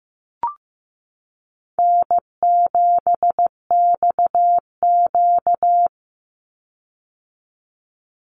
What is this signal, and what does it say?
Morse code, heard as a keyed tone, about 15 wpm, 700 Hz. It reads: N7XQ
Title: USA Call Signs 15wpm